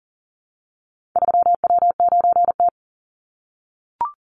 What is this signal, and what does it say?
Morse code, heard as a keyed tone, about 40 wpm, 700 Hz. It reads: 3W9T
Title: Call Signs - 40wpm